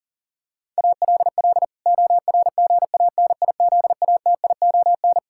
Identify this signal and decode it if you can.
Morse code, heard as a keyed tone, about 40 wpm, 700 Hz. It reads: ALL ORGANIZATION
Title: Sentences from Top 1000 Words - Review Part 5 - 40wpm